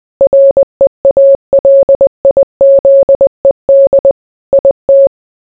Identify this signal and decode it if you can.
Morse code, heard as a keyed tone, about 20 wpm, 550 Hz. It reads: REALIZED IT